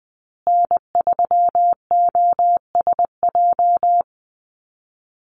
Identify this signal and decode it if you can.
Morse code, heard as a keyed tone, about 20 wpm, 700 Hz. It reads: N3OSJ
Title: USA Call Signs Repeated 2x 20wpm